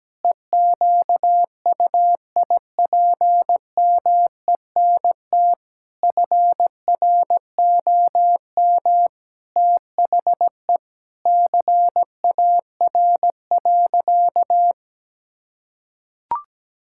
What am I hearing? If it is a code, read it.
Morse code, heard as a keyed tone, about 17 wpm, 700 Hz. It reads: EQUIPMENT FROM THE CAR.
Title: Sentences from Top 1000 Words - Review Part 5 - 17wpm